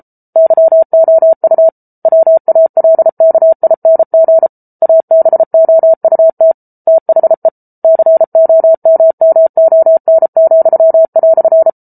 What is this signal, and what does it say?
Morse code, heard as a keyed tone, about 34 wpm, 650 Hz. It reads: YOU WALKING ABOUT THE COMMON,"